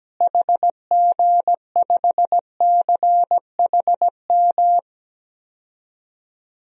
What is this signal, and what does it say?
Morse code, heard as a keyed tone, about 17 wpm, 700 Hz. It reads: HG5CHM
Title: DXCC-100 Call Signs - 6 Characters - Repeated 3x - 17wpm